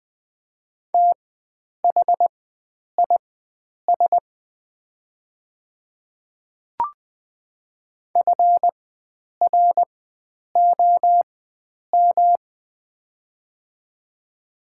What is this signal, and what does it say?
Morse code, heard as a keyed tone, about 20 wpm, spaced as for 9 wpm, 700 Hz. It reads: THIS FROM
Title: 4x Character Spacing - Top 500 Words - 20wpm